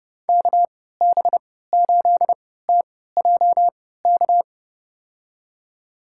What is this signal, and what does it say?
Morse code, heard as a keyed tone, about 32 wpm, spaced as for 17 wpm, 700 Hz. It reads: KB8TJK
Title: USA Call Signs - 6 Characters - 3x Character Spacing - 30wpm